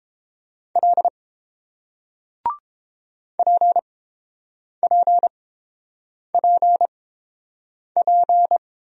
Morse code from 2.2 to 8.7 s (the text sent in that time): PPPP